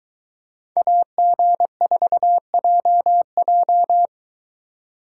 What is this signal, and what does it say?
Morse code, heard as a keyed tone, about 23 wpm, 700 Hz. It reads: AG4JJ